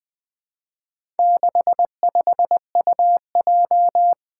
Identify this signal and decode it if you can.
Morse code, heard as a keyed tone, about 20 wpm, 700 Hz. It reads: 65UJ
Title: Four Challenging Letter-Number Combinations 20wpm